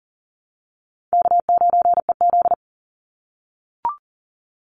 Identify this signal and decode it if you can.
Morse code, heard as a keyed tone, about 40 wpm, 700 Hz. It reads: K9EZ
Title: NA SOTA Chaser Callsigns 40wpm